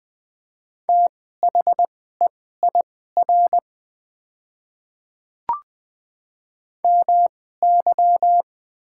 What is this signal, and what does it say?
Morse code, heard as a keyed tone, about 20 wpm, spaced as for 14 wpm, 700 Hz. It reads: THEIR MY
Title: 2x Character Spacing - Top 100 Words - 20wpm